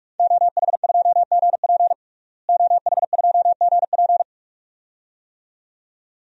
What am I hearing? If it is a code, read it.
Morse code, heard as a keyed tone, about 42 wpm, 700 Hz. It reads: OH2GP OH2GP
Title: DXCC-100 Call Signs - 5 Characters - Repeated 2x - 45wpm